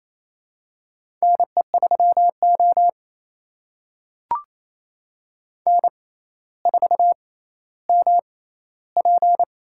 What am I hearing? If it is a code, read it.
Morse code, heard as a keyed tone, about 28 wpm, spaced as for 14 wpm, 700 Hz. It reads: NE3O N4MP
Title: USA Call Signs - 4 Characters - 6x Character Spacing - 28wpm